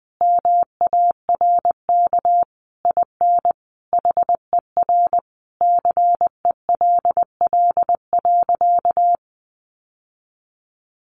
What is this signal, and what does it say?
Morse code, heard as a keyed tone, about 20 wpm, 700 Hz. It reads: MARK IN HER CELL.
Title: Sentences from Top 1100 Words 20wpm